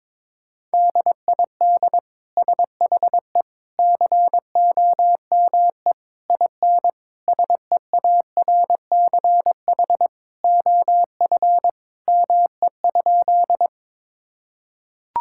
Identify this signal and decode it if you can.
Morse code, heard as a keyed tone, about 22 wpm, 700 Hz. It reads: DID SHE COME IN SEARCH OF ME?
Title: Sentences from Top 1000 Words - Review Part 5 - 22wpm